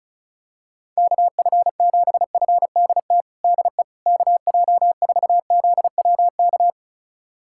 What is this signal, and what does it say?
Morse code, heard as a keyed tone, about 34 wpm, 700 Hz. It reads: KF7FDT DE KJ4ZWK
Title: USA Callsign DE Callsign - 6 Characters - 35wpm